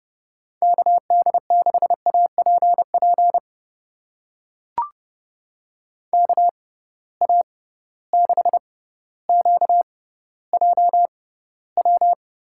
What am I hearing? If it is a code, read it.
Morse code, heard as a keyed tone, about 31 wpm, spaced as for 15 wpm, 700 Hz. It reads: KD6APP KA6QJW
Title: USA Call Signs - 6 Characters - 6x Character Spacing - 30wpm